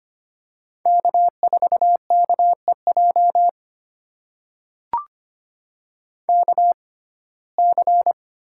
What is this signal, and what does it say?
Morse code, heard as a keyed tone, about 25 wpm, spaced as for 18 wpm, 700 Hz. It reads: K4KEJ KC
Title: USA Call Signs - 6x Character Spacing - 25wpm